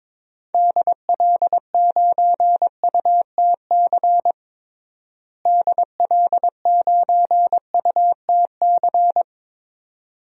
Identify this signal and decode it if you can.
Morse code, heard as a keyed tone, about 22 wpm, 700 Hz. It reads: DL9UTC DL9UTC
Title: DXCC-100 Call Signs - 6 Characters - Repeated 3x - 22wpm